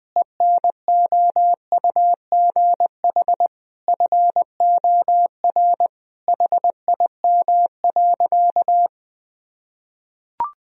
Morse code, heard as a keyed tone, about 20 wpm, 700 Hz. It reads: ENOUGH FOR HIM.
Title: Sentences from Top 1000 Words - Review Part 2 - 20wpm